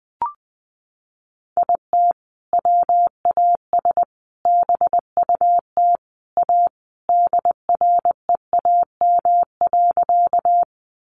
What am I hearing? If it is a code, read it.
Morse code, heard as a keyed tone, about 19 wpm, 700 Hz. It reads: IT WAS BUT A DREAM.
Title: Sentences from Top 1500 Words 20wpm